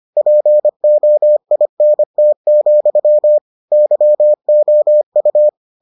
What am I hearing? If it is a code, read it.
Morse code, heard as a keyed tone, about 25 wpm, 600 Hz. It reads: POINT, YOU